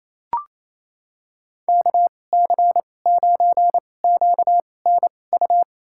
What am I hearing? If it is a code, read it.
Morse code, heard as a keyed tone, about 28 wpm, spaced as for 20 wpm, 700 Hz. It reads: KC9QNU
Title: USA Call Signs - 6 Characters - 2x Character Spacing - 28wpm